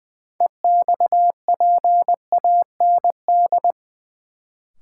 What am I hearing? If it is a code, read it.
Morse code, heard as a keyed tone, about 21 wpm, 700 Hz. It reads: EXPAND
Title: Sentences from Top 1400 Words - New Word Drill Alt - 20wpm